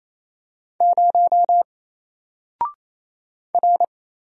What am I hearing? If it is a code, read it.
Morse code, heard as a keyed tone, about 27 wpm, 700 Hz. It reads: 0 R
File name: Single-Letter-Number-28wpm